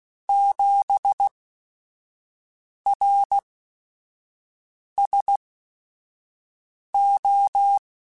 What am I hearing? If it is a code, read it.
Morse code, heard as a keyed tone, about 16 wpm, spaced as for 5 wpm, 800 Hz. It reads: 7 RSO
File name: QSOs / QSOs-mp3s / WA7QOX 1@5